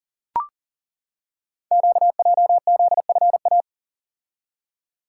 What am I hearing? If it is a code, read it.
Morse code, heard as a keyed tone, about 41 wpm, 700 Hz. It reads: QJZFA
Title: Code Groups - 5 Letters - 40wpm